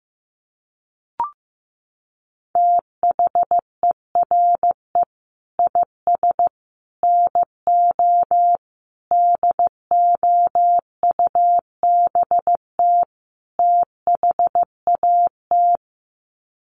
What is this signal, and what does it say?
Morse code, heard as a keyed tone, about 14 wpm, 700 Hz. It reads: THERE IS NO DOUBT THAT
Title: Sets of 5 Words 15wpm